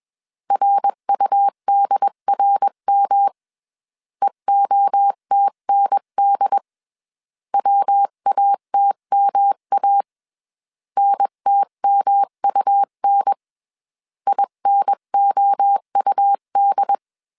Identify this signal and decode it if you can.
Morse code, heard as a keyed tone, about 21 wpm, 800 Hz. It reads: RUDRM EOTND WATMA NTMUN INOUD
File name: 20wpm